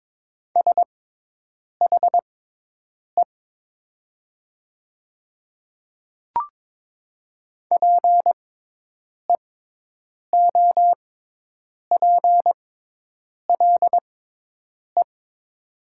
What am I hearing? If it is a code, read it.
Morse code, heard as a keyed tone, about 22 wpm, spaced as for 8 wpm, 700 Hz. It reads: SHE PEOPLE
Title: 6x Character Spacing - Top 100 Words - 22wpm